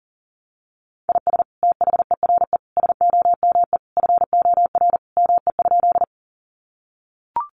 Morse code, heard as a keyed tone, about 40 wpm, 700 Hz. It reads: IS THERE SOME FOR ME?
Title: Sentences from Top 200 Words 40wpm